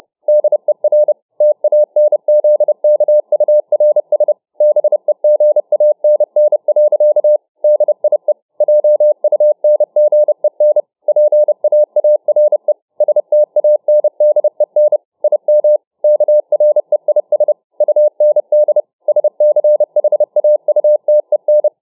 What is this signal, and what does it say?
Morse code, heard as a keyed tone, about 30 wpm, 600 Hz. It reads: DER TANZKURS BEGANN. DIE JUNGEN PAARE STANDEN IM KREIS UND SCHAUTEN